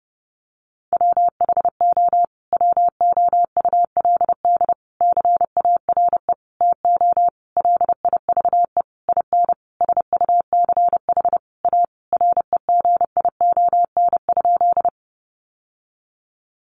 Morse code, heard as a keyed tone, about 30 wpm, 700 Hz. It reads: WHO WOULD CARE TO LIVE IN SUCH A REGION?
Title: Sentences from Top 1500 Words 30wpm